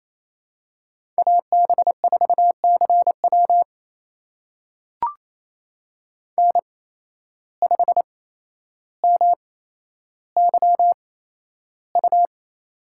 Morse code, heard as a keyed tone, about 28 wpm, spaced as for 12 wpm, 700 Hz. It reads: AB4CW N5MYU
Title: USA Call Signs - 8x Character Spacing - 28wpm